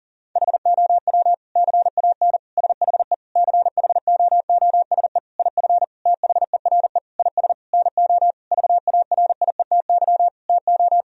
Morse code, heard as a keyed tone, about 40 wpm, 700 Hz. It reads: HOW CAN SHE CHOOSE IF THERE IS NO VARIETY TO
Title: Sentences from Top 1000 Words - Review Part 4 - 40wpm